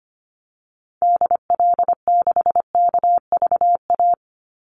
Morse code, heard as a keyed tone, about 25 wpm, 700 Hz. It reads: DL6KVA
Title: Call Signs Repeated 2x - 25wpm